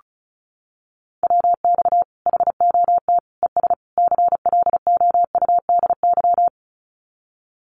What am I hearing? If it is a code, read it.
Morse code, heard as a keyed tone, about 34 wpm, 700 Hz. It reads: WX HOT ES CLOUDY
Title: Common QSO Phrases 35wpm